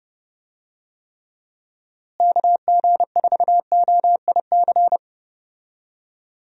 Morse code, heard as a keyed tone, about 30 wpm, 700 Hz. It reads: KG4OIC
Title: USA Call Signs - 6 Characters - Repeated 2x - 30wpm